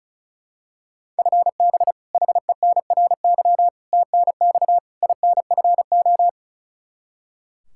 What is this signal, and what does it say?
Morse code, heard as a keyed tone, about 34 wpm, 700 Hz. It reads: FB HENRY TNX INFO
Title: Practice QSO - Exchange 3 - More Info - 35wpm